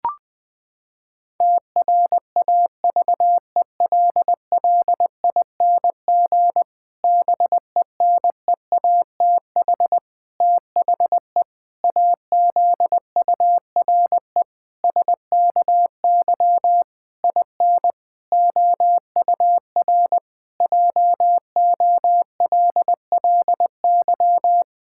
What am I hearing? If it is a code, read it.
Morse code, heard as a keyed tone, about 20 wpm, 700 Hz. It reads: TRAVELLING BENEATH THE AZURE SKY IN OUR JOLLY
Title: Pangram Challenge v2 - 20wpm